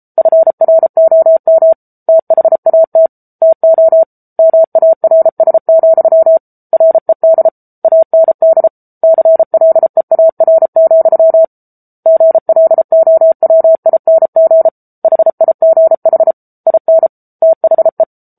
Morse code, heard as a keyed tone, about 34 wpm, 650 Hz. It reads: FROM THAT TO MARS, RED AND CLEAR, GLOWING HIGH IN THE
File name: Book Two / 07-Chapter 7 - 33 WPM